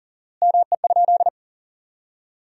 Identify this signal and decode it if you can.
Morse code, heard as a keyed tone, about 40 wpm, 700 Hz. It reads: ME?
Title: 4x Word-Spacing - Sentences from Top 400 Words - 40wpm